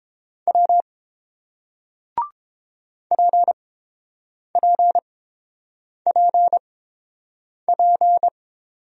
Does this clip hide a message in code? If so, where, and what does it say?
Yes: Morse code from 0.5 to 8.3 s, heard as a keyed tone, about 30 wpm, spaced as for 7 wpm, 700 Hz. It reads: W PPPP